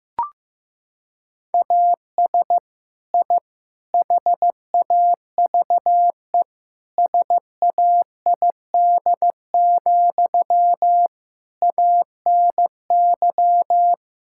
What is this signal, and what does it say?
Morse code, heard as a keyed tone, about 16 wpm, 700 Hz. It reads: AS I HAVE SAID, ANY
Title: Sentences from Top 1000 Words - Review Part 6 - 15wpm